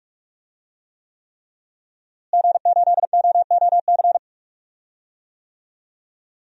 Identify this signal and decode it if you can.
Morse code, heard as a keyed tone, about 47 wpm, 700 Hz. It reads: G8OOC
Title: DXCC-100 Call Signs - 5 Characters - Repeated 3x - 45wpm